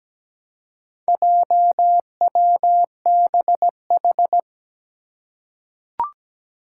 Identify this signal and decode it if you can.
Morse code, heard as a keyed tone, about 17 wpm, 700 Hz. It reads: JWBH
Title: Four Challenging Letter-Number Combinations 17wpm